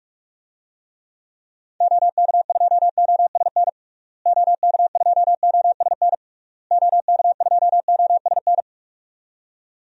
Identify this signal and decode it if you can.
Morse code, heard as a keyed tone, about 46 wpm, 700 Hz. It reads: OK2OSN OK2OSN OK2OSN
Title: DXCC-100 Call Signs - 6 Characters - Repeated 3x - 45wpm